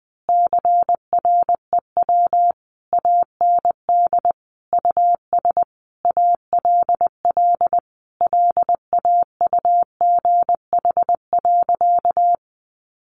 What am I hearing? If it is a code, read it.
Morse code, heard as a keyed tone, about 20 wpm, 700 Hz. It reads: CREW AND US ALL LAUGH.
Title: Sentences from Top 1500 Words 20wpm